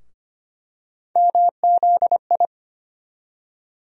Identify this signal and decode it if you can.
Morse code, heard as a keyed tone, about 25 wpm, 700 Hz. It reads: MZI
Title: USA Call Sign Postfixes - 3 Letters 25wpm